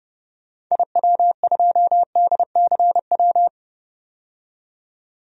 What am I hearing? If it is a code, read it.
Morse code, heard as a keyed tone, about 32 wpm, 700 Hz. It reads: IW2DCW
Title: DXCC-100 Call Signs - 6 Characters - 30wpm